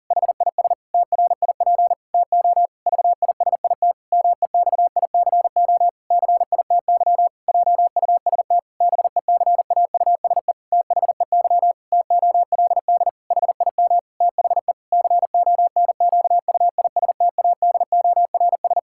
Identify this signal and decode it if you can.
Morse code, heard as a keyed tone, about 37 wpm, 700 Hz. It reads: HIS TRIP TO VISIT MEXICO CITY JUST BECAUSE THEY TOLD HIM THE CONQUISTADORS
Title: Pangram Challenge v2 - 40wpm